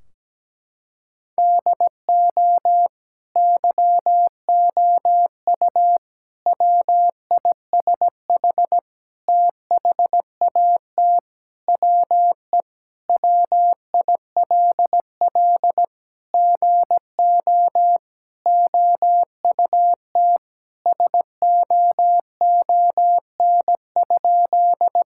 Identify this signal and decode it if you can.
Morse code, heard as a keyed tone, about 17 wpm, 700 Hz. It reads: DO YOU WISH THAT WE WILL GO OUT SOON?
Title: Sentences from Top 1600 Words 17wpm